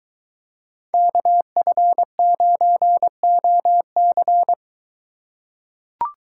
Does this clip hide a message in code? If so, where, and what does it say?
Yes: Morse code from 0.9 to 4.5 s, heard as a keyed tone, about 22 wpm, 700 Hz. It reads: KF9OC